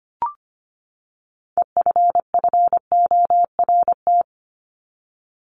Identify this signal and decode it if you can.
Morse code, heard as a keyed tone, about 25 wpm, 700 Hz. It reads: EFFORT
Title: Top 1001-1250 Words 25wpm